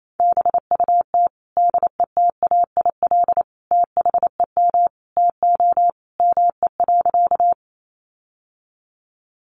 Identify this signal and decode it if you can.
Morse code, heard as a keyed tone, about 28 wpm, 700 Hz. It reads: BUT DETAIL THEM TO ME.
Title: Sentences from Top 1500 Words 28wpm